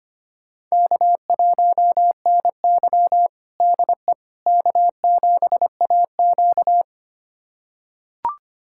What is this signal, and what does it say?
Morse code, heard as a keyed tone, about 25 wpm, 700 Hz. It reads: K1NY DE K7AQ
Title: USA Callsign DE Callsign - 4 Characters - 25wpm